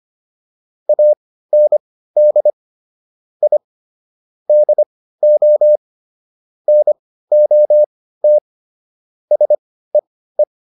Morse code, heard as a keyed tone, about 25 wpm, spaced as for 15 wpm, 600 Hz. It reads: AND I DO NOT SEE
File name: WD404 15